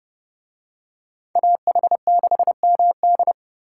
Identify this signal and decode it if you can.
Morse code, heard as a keyed tone, about 28 wpm, 700 Hz. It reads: AH6MD